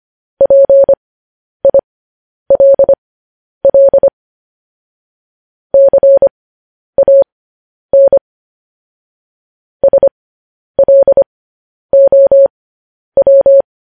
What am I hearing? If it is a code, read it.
Morse code, heard as a keyed tone, about 25 wpm, spaced as for 10 wpm, 550 Hz. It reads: PILL CAN SLOW
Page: news headlines in morse code at 10 wpm